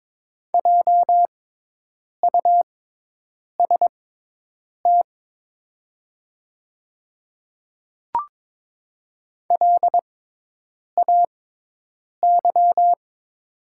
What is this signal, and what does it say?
Morse code, heard as a keyed tone, about 22 wpm, spaced as for 8 wpm, 700 Hz. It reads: JUST LAY
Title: 6x Character Spacing - Top 500 Words - 22wpm